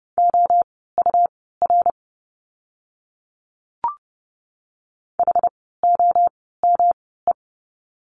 Morse code, heard as a keyed tone, about 30 wpm, spaced as for 17 wpm, 700 Hz. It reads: OUR HOME
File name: course / 30:10wpm / 115 Lesson 19f - Letter w - Words 30wpm